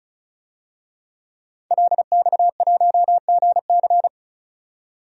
Morse code, heard as a keyed tone, about 34 wpm, 700 Hz. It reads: LX1GC
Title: DXCC-100 Call Signs - 5 Characters - 35wpm